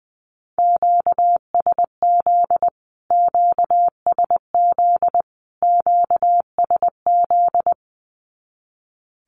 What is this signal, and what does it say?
Morse code, heard as a keyed tone, about 20 wpm, 700 Hz. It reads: QSZ QSZ QSZ
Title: Q-Codes Repeated 3x 20wpm